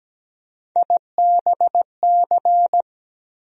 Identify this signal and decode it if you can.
Morse code, heard as a keyed tone, about 17 wpm, 700 Hz. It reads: IBC
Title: USA Call Sign Postfixes - 3 Letters 17wpm